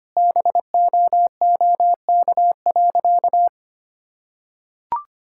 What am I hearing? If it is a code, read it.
Morse code, heard as a keyed tone, about 25 wpm, 700 Hz. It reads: BOOK.
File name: practice / 25wpm / Sentences with 2-Word Phrases - 25wpm